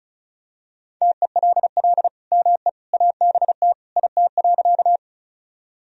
Morse code, heard as a keyed tone, about 34 wpm, 690 Hz. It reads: TELL ME ABT IT.
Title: Sentences from Top 200 Words and Abbreviations 35wpm